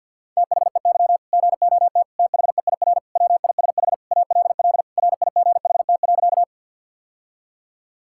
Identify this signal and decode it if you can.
Morse code, heard as a keyed tone, about 51 wpm, 700 Hz. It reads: THEY GOT THEIR WISH ALL RIGHT.
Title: Sentences from Top 1600 Words 50wpm